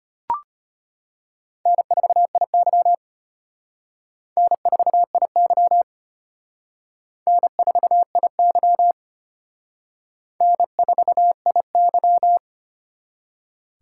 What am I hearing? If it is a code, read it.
Morse code, heard as a keyed tone, about 31 wpm, 700 Hz. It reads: N4IY N4IY N4IY N4IY